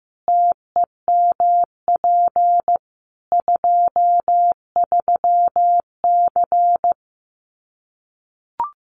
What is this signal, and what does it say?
Morse code, heard as a keyed tone, about 15 wpm, 700 Hz. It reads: TEMP 23C
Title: Common QSO Phrases 15wpm